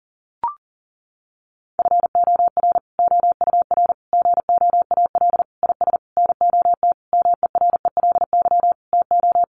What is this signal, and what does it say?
Morse code, heard as a keyed tone, about 40 wpm, 700 Hz. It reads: FOR OUR GOAL IS NOT MERELY TO
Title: Sentences from Top 1400 Words 40wpm